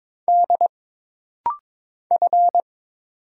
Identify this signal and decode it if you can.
Morse code, heard as a keyed tone, about 22 wpm, 700 Hz. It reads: D F
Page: This rapid-fire timing is geared to push your ICR (Instant Character Recognition) skills.